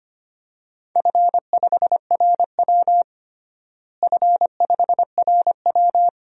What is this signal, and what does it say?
Morse code, heard as a keyed tone, about 25 wpm, 700 Hz. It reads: F5RW F5RW
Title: DXCC-100 Call Signs - 4 Characters - Repeated 3x - 25wpm